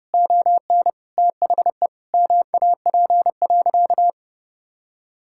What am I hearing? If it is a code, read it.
Morse code, heard as a keyed tone, about 30 wpm, 700 Hz. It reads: ON THE MAP.
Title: Sentences from Top 1000 Words - Review Part 6 - 30wpm